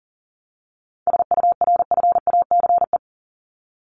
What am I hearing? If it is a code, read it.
Morse code, heard as a keyed tone, about 37 wpm, 700 Hz. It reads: SURFACE
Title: Top 500 Words Repeated 40wpm